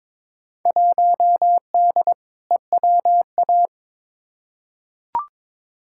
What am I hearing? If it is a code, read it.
Morse code, heard as a keyed tone, about 22 wpm, 700 Hz. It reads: 1D EWA
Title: Field Day Exchanges - 22wpm